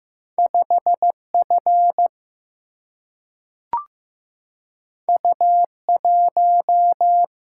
Morse code, heard as a keyed tone, about 15 wpm, 700 Hz. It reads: 5F U1
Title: Single-Letter Single-Number 15wpm